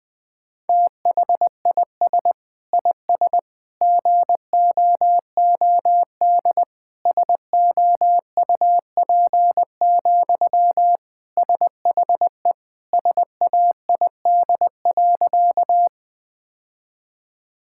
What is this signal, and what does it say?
Morse code, heard as a keyed tone, about 20 wpm, 700 Hz. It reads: THIS IS GOOD SOUP, SHE SAID.
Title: Sentences from Top 1000 Words - Review Part 6 - 20wpm